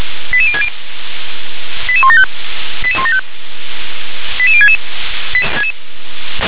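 Five-tone selective call (ZVEI), heard as quick runs of five tones, tones of about 70 ms. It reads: 88060 88175 88175 88066
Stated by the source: ZVEI five-tone selective call received by SDR.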